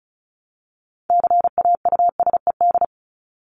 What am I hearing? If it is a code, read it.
Morse code, heard as a keyed tone, about 34 wpm, 700 Hz. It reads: CAUSED
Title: ED Suffix - Top 100 Words with Base Words 35wpm